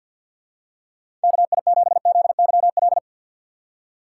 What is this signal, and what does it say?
Morse code, heard as a keyed tone, about 49 wpm, 700 Hz. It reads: KI7ZYL
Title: USA Call Signs - 4x Character Spacing - 50wpm